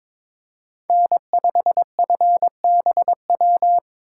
Morse code, heard as a keyed tone, about 25 wpm, 700 Hz. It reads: N5FBW
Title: Speed Racing - USA 1x3 Call Signs 22wpm